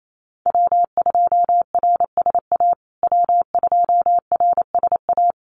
Morse code, heard as a keyed tone, about 28 wpm, 700 Hz. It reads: W2RSA W2RSA
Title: USA Call Signs Repeated 3x 28wpm